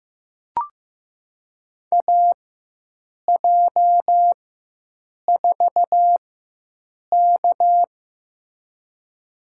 Morse code, heard as a keyed tone, about 15 wpm, spaced as for 7 wpm, 700 Hz. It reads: AJ4K
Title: USA Call Signs - 4 Characters - 4x Character Spacing - 15wpm